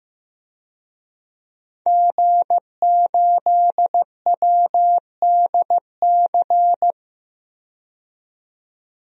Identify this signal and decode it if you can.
Morse code, heard as a keyed tone, about 15 wpm, 700 Hz. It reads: G8WDC
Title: DXCC-100 Call Signs - 5 Characters - 15wpm